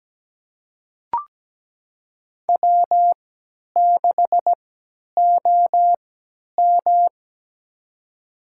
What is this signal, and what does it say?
Morse code, heard as a keyed tone, about 17 wpm, spaced as for 10 wpm, 700 Hz. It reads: W6OM
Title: USA Call Signs - 4 Characters - 3x Character Spacing - 17wpm